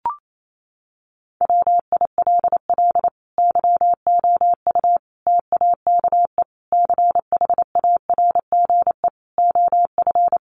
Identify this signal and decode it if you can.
Morse code, heard as a keyed tone, about 28 wpm, 700 Hz. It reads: WILL YOU TAKE CHARGE OF
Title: Sentences from Top 900 Words 28wpm